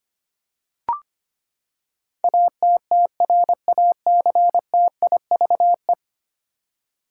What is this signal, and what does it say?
Morse code, heard as a keyed tone, about 25 wpm, 700 Hz. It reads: ATTRACTIVE
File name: Ten Letter Words - 25wpm